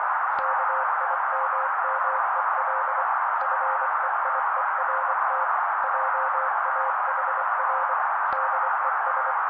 Morse code, heard as a keyed tone, about 24 wpm, 530 Hz. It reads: KIMMEL FEIERT JAHR DES
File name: daily / 23wpm noise